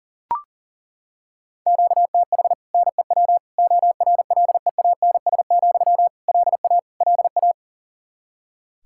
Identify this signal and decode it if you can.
Morse code, heard as a keyed tone, about 40 wpm, 700 Hz. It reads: QTH NEW ORLEANS, LA LA
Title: Practice QSO - Exchange 1 - Beginning - 40wpm